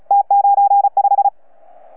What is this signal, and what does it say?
Morse code, heard as a keyed tone, about 35 wpm, 800 Hz. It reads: T95
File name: UP2L-20-CW